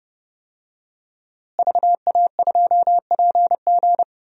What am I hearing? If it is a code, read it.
Morse code, heard as a keyed tone, about 32 wpm, 700 Hz. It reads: VA2PG
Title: DXCC-100 Call Signs - 5 Characters - Repeated 3x - 30wpm